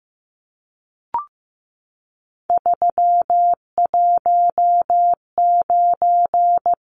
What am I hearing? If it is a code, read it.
Morse code, heard as a keyed tone, about 15 wpm, 700 Hz. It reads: 319
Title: Numbers - 3-Digit 15wpm